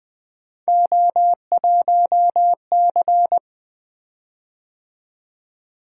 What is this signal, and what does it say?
Morse code, heard as a keyed tone, about 20 wpm, 700 Hz. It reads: O1C
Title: Three Letter-Number Combinations - 20wpm